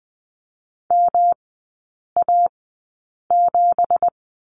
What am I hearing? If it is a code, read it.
Morse code, heard as a keyed tone, about 19 wpm, spaced as for 8 wpm, 700 Hz. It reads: MA7
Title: 205 Lesson 33e - Letter j - Three Characters 20wpm